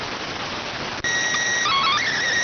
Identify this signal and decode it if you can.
SSTV (slow-scan television), PD 290.